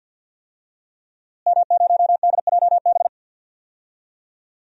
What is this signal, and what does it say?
Morse code, heard as a keyed tone, about 47 wpm, 700 Hz. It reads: M0DJB
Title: DXCC-100 Call Signs - 5 Characters - Repeated 3x - 50wpm